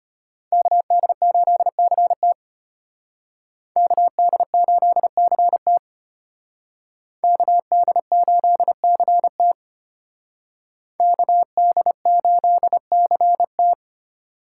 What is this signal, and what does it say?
Morse code, heard as a keyed tone, about 32 wpm, 700 Hz. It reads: KD8CT KD8CT KD8CT KD8CT